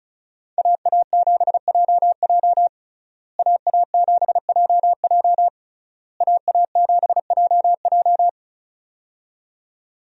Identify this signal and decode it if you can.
Morse code, heard as a keyed tone, about 34 wpm, 700 Hz. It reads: AA7JJ AA7JJ AA7JJ
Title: USA Call Signs - 5 Characters - Repeated 3x - 35wpm